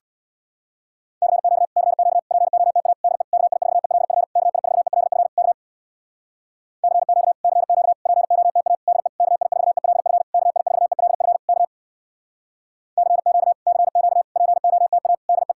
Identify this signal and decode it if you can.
Morse code, heard as a keyed tone, about 62 wpm, 700 Hz. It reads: CQ CQ POTA DE KI4ELF KI4ELF K CQ CQ POTA DE KI4ELF KI4ELF K CQ CQ POTA DE
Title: Speed-Racing Common QSO Phrases - 50wpm